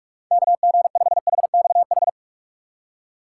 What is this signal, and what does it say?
Morse code, heard as a keyed tone, about 45 wpm, 700 Hz. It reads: KG5HXH
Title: USA Call Signs - 6 Characters - 6x Character Spacing - 45wpm